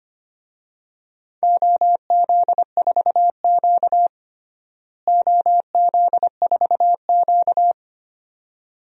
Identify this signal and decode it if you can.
Morse code, heard as a keyed tone, about 25 wpm, 700 Hz. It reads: OZ4Q OZ4Q
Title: DXCC-100 Call Signs - 4 Characters - Repeated 2x - 25wpm